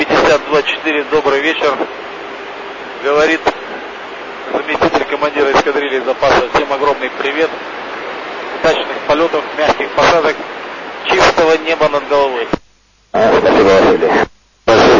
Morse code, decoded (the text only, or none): none